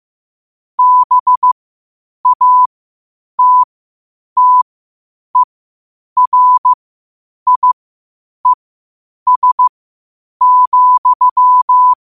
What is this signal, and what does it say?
Morse code, heard as a keyed tone, about 15 wpm, spaced as for 8 wpm, 1000 Hz. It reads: BATTERIES,
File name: qotdfiles / 8wpm